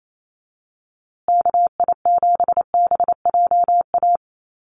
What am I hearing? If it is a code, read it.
Morse code, heard as a keyed tone, about 28 wpm, 700 Hz. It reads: KI7BJA
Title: USA Call Signs Repeated 3x 28wpm